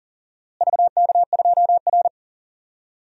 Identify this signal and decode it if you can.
Morse code, heard as a keyed tone, about 37 wpm, 700 Hz. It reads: VK2R
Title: DXCC-100 Call Signs - 4 Characters - Repeated 3x - 40wpm